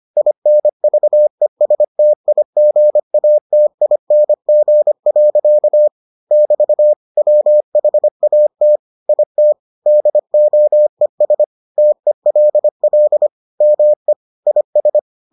Morse code, heard as a keyed tone, about 25 wpm, 600 Hz. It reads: INVESTIGATING. <BT> WHAT IT DOES TELL ME IS